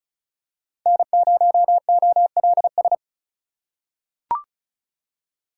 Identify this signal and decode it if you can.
Morse code, heard as a keyed tone, about 36 wpm, 700 Hz. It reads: N0OLS
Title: USA Call Signs - 4x Character Spacing - 35wpm